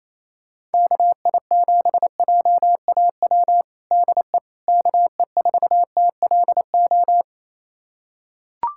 Morse code, heard as a keyed tone, about 28 wpm, 700 Hz. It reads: KI7JAW DE KE4TLO
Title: USA Callsign DE Callsign 28wpm